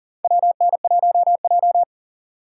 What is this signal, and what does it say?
Morse code, heard as a keyed tone, about 40 wpm, 700 Hz. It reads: WN1J
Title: USA Call Signs - 3x Character Spacing - 40wpm